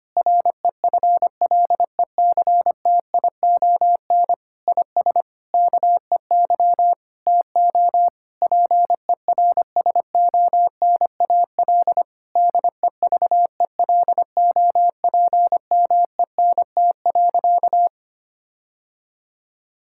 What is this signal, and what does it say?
Morse code, heard as a keyed tone, about 25 wpm, 700 Hz. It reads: REFLECTION IS KEY TO PERSONAL DEVELOPMENT.
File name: practice / 25wpm / Difficult Sentences with Hints - 25wpm